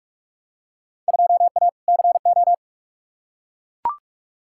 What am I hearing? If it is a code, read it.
Morse code, heard as a keyed tone, about 43 wpm, 700 Hz. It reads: 2A CO
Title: Field Day Exchanges - 45wpm